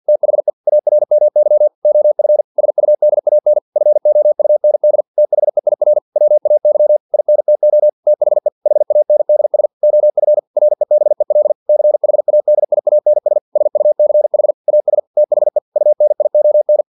Morse code, heard as a keyed tone, about 50 wpm, 600 Hz. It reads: THE ARMY OF SUDAN FOUND THEIR WAY INTO THE HANDS OF REBEL CHADIANS SUCH AS THE UNION
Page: Never 49wpm